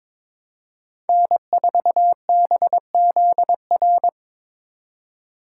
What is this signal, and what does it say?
Morse code, heard as a keyed tone, about 23 wpm, 700 Hz. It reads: N4BZR
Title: USA Call Signs - 5 Characters - 3x Character Spacing - 22wpm